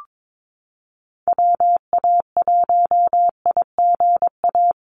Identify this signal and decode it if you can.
Morse code, heard as a keyed tone, about 22 wpm, 700 Hz. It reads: WA1IGA
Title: USA Call Signs Repeated 3x 22wpm